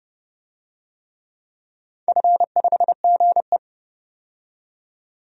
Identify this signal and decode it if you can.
Morse code, heard as a keyed tone, about 30 wpm, 700 Hz. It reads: F5GE
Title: DXCC-100 Call Signs - 4 Characters - 30wpm